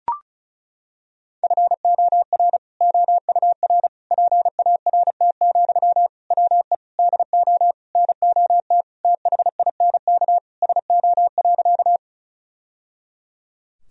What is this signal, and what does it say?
Morse code, heard as a keyed tone, about 34 wpm, 700 Hz. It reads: FOR OUR PART, WE DO NOT THINK SO.
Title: Sentences from Top 1000 Words - Review Part 1 - 35wpm